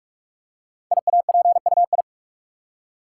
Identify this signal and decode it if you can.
Morse code, heard as a keyed tone, about 44 wpm, 690 Hz. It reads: IAPUI
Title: Code Groups - 5 Letters - 45wpm